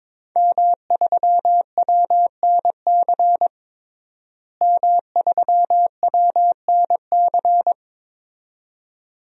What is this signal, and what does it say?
Morse code, heard as a keyed tone, about 22 wpm, 700 Hz. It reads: M3WNC M3WNC
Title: DXCC-100 Call Signs - 5 Characters - Repeated 2x - 22wpm